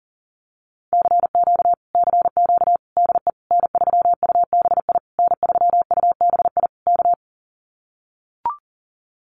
Morse code, heard as a keyed tone, about 40 wpm, 700 Hz. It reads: CQ CQ DE N3UBI N3UBI K
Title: Common QSO Phrases Encore 40wpm